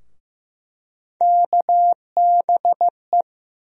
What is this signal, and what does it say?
Morse code, heard as a keyed tone, about 15 wpm, 700 Hz. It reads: KBE
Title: USA Call Sign Postfixes - 3 Letters 15wpm